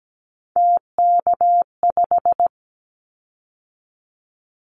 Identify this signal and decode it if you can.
Morse code, heard as a keyed tone, about 17 wpm, 700 Hz. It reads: TK5
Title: Callsign Prefixes for Top 100 DXCC - 17wpm